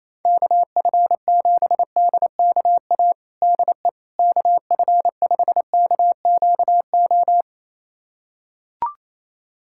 Morse code, heard as a keyed tone, about 28 wpm, 700 Hz. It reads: KF7DKA DE KF5KQO
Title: USA Callsign DE Callsign - 6 Characters - 28wpm